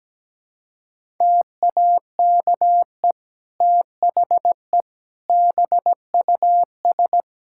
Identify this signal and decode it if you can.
Morse code, heard as a keyed tone, about 17 wpm, 700 Hz. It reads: TAKE THE BUS
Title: Sentences from Top 1000 Words - Review Part 6 - 17wpm